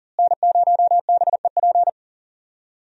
Morse code, heard as a keyed tone, about 40 wpm, 700 Hz. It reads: N0BEP
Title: USA Call Signs - 5 Characters - Repeated 3x - 40wpm